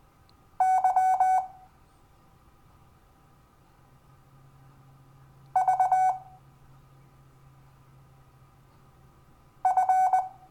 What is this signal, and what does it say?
Morse code, heard as a keyed tone, about 20 wpm, 750 Hz. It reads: Y V F
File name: cw / Lesson12 YV